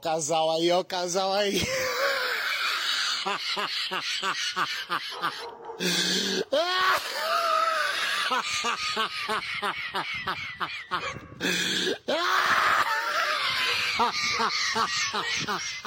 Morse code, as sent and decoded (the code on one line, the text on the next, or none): none